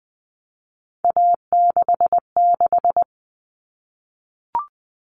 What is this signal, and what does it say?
Morse code, heard as a keyed tone, about 20 wpm, 700 Hz. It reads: A66
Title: All Cut Numbers - 3-Digit 20wpm